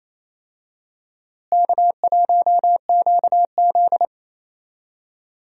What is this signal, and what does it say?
Morse code, heard as a keyed tone, about 27 wpm, 700 Hz. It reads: K1QZ
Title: USA Call Signs - 4 Characters - Repeated 2x - 28wpm